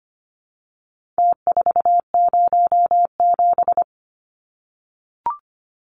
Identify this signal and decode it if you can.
Morse code, heard as a keyed tone, about 25 wpm, 700 Hz. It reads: T407
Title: Numbers - Contest Style - 4-Digit 25wpm